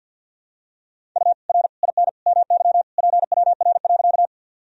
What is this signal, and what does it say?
Morse code, heard as a keyed tone, about 49 wpm, 700 Hz. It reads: U R IN MY PWR.
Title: Sentences from Top 500 Words and Abbreviations 50wpm